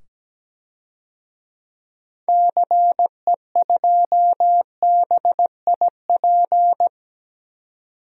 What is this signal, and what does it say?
Morse code, heard as a keyed tone, about 17 wpm, 700 Hz. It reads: CE2BIP
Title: DXCC-100 Call Signs - 6 Characters - 17wpm